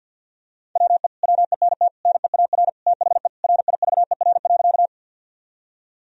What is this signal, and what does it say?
Morse code, heard as a keyed tone, about 51 wpm, 700 Hz. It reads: WE WENT NEAR THE RIVER.
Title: Sentences from Top 1000 Words - Review Part 1 - 50wpm